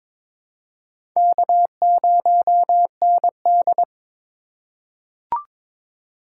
Morse code, heard as a keyed tone, about 22 wpm, 700 Hz. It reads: K0ND
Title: USA Call Signs - 4 Characters - 3x Character Spacing - 22wpm